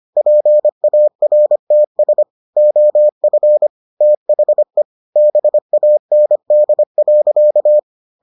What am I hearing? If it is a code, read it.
Morse code, heard as a keyed tone, about 25 wpm, 600 Hz. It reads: PARTS OF THE BAND.